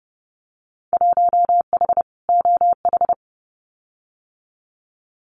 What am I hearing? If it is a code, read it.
Morse code, heard as a keyed tone, about 30 wpm, 700 Hz. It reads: 1H OH
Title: Winter Field Day Exchanges 30wpm